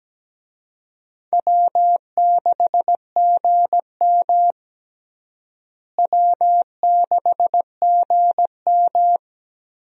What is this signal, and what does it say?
Morse code, heard as a keyed tone, about 17 wpm, 700 Hz. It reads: W6GM W6GM
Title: USA Call Signs - 4 Characters - Repeated 2x - 17wpm